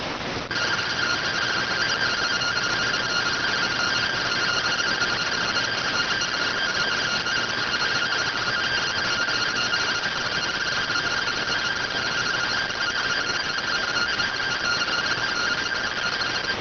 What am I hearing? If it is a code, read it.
Two-tone teleprinter signal, about 170 Hz shift.